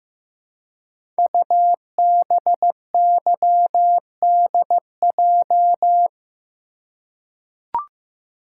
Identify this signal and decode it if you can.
Morse code, heard as a keyed tone, about 15 wpm, 700 Hz. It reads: UBYDJ
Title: Code Groups - 5 Letters - 15wpm